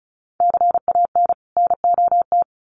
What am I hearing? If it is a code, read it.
Morse code, heard as a keyed tone, about 34 wpm, 700 Hz. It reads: CAN NOT
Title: Sets of 2 Words 35wpm